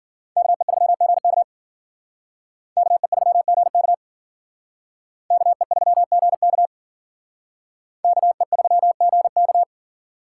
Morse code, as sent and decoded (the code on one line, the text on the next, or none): -.- . ...-- --. -.- / -.- . ...-- --. -.- / -.- . ...-- --. -.- / -.- . ...-- --. -.-
KE3GK KE3GK KE3GK KE3GK